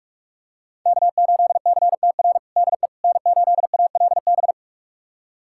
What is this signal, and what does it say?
Morse code, heard as a keyed tone, about 43 wpm, 700 Hz. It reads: K8CTR DE N8ALB
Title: USA Callsign DE Callsign - 5 Characters - 45wpm